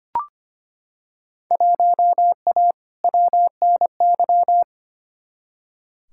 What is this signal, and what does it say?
Morse code, heard as a keyed tone, about 25 wpm, 700 Hz. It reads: 1A WNY
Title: Field Day Exchanges - 25wpm